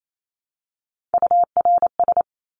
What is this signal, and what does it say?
Morse code, heard as a keyed tone, about 28 wpm, 700 Hz. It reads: URS
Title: Abbreviations Repeated 3x - 28wpm